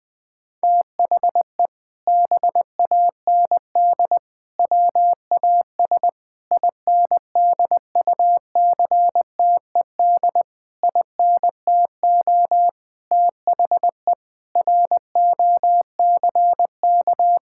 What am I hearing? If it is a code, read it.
Morse code, heard as a keyed tone, about 21 wpm, 700 Hz. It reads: THE BAND WAS INDUCTED INTO THE ROCK
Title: Sentences with 6-Word Phrases - Spoken Beginning - 20wpm